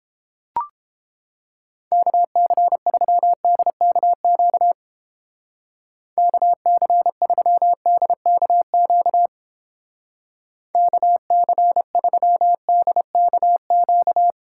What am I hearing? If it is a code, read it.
Morse code, heard as a keyed tone, about 29 wpm, 700 Hz. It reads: KC3DKQ KC3DKQ KC3DKQ